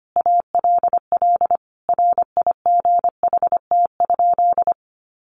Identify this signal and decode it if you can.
Morse code, heard as a keyed tone, about 25 wpm, 700 Hz. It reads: ALL RIGHT?
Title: Sentences from Top 700 Words 25wpm